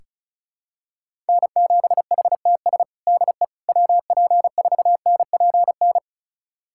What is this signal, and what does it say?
Morse code, heard as a keyed tone, about 36 wpm, 700 Hz. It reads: N7HTS DE WP4NPN
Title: USA Callsign DE Callsign 35wpm